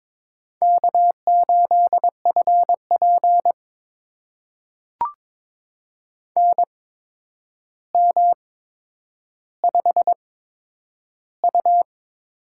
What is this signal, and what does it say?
Morse code, heard as a keyed tone, about 22 wpm, spaced as for 9 wpm, 700 Hz. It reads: K8FP NM5U
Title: USA Call Signs - 4 Characters - 8x Character Spacing - 22wpm